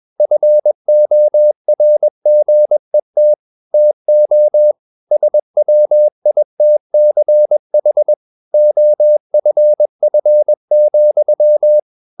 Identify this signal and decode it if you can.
Morse code, heard as a keyed tone, about 21 wpm, 600 Hz. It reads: FORGET TO SWITCH OFF,
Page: CW med Gnister